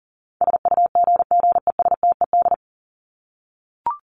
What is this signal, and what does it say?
Morse code, heard as a keyed tone, about 40 wpm, 700 Hz. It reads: SUGGESTED
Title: ED Suffix - Top 100 Words with Base Words 40wpm